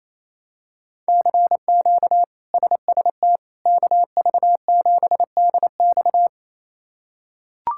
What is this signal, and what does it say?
Morse code, heard as a keyed tone, about 28 wpm, 700 Hz. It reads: CQ SST KV7DX
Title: K1USN SST QSO Practice - 28wpm